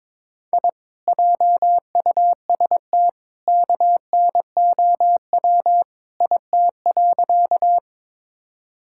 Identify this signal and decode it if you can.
Morse code, heard as a keyed tone, about 22 wpm, 700 Hz. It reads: I JUST KNOW IT.
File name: practice / 22wpm / Sentences with 2-Word Phrases - 22wpm